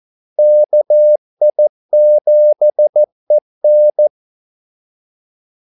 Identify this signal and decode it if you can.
Morse code, heard as a keyed tone, about 14 wpm, 600 Hz. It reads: KI7EN
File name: Callsigns-14wpm